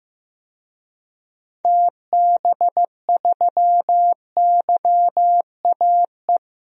Morse code, heard as a keyed tone, about 16 wpm, 700 Hz. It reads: TB3YAE
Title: DXCC-100 Call Signs - 6 Characters - 15wpm